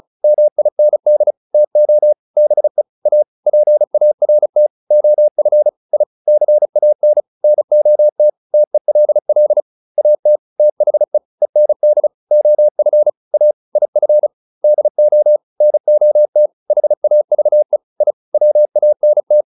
Morse code, heard as a keyed tone, about 34 wpm, 600 Hz. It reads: MIND TO BE A PART OF I CAN NOT TELL AT THE END OF A IF DO NOT HAVE I WANT